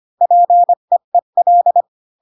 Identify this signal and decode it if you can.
Morse code, heard as a keyed tone, about 25 wpm, 700 Hz. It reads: PEEL
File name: Pinocchio / Pinocchio ch1 20 25 ik1hge